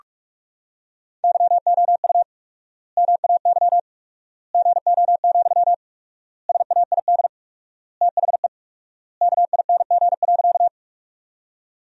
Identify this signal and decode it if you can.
Morse code, heard as a keyed tone, about 46 wpm, 700 Hz. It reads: YOU MAY GO, SAID THE KING.
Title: 4x Word-Spacing - Sentences from Top 400 Words - 45wpm